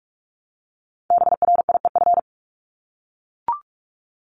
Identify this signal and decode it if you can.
Morse code, heard as a keyed tone, about 44 wpm, 700 Hz. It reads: BRIEF
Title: LY Suffix - Top 100 Words with Base Words 45wpm